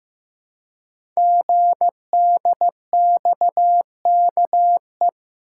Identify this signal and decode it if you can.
Morse code, heard as a keyed tone, about 15 wpm, 700 Hz. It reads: GDXKE
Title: Code Groups - 5 Letters - 15wpm